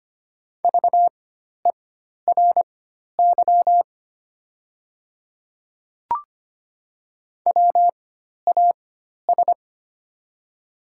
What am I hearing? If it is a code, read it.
Morse code, heard as a keyed tone, about 25 wpm, spaced as for 12 wpm, 690 Hz. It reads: VERY WAS
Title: 4x Character Spacing - Top 200 Words - 25wpm